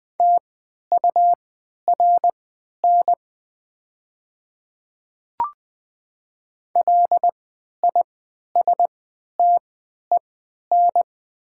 Morse code, heard as a keyed tone, about 20 wpm, spaced as for 11 wpm, 700 Hz. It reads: TURN LISTEN
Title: 3x Character Spacing - Top 500 Words - 20wpm